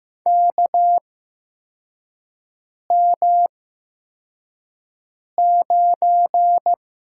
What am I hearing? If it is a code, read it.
Morse code, heard as a keyed tone, about 15 wpm, spaced as for 4 wpm, 700 Hz. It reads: KM9
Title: USA Call Signs - 8x Character Spacing - 15wpm